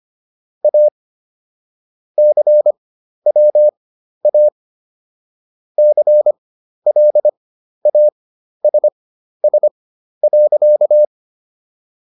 Morse code, heard as a keyed tone, about 25 wpm, spaced as for 12 wpm, 600 Hz. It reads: A CWA CLASS.